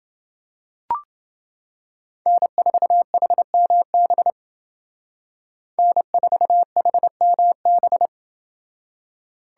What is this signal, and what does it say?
Morse code, heard as a keyed tone, about 28 wpm, 700 Hz. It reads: N4HMB N4HMB